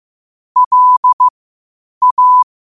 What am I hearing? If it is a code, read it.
Morse code, heard as a keyed tone, about 15 wpm, spaced as for 8 wpm, 1000 Hz. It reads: LA
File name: qotdfiles / 8wpm